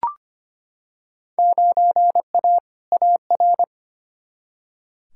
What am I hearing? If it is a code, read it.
Morse code, heard as a keyed tone, about 25 wpm, 700 Hz. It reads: 9A AR
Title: Field Day Exchanges Encore - 25wpm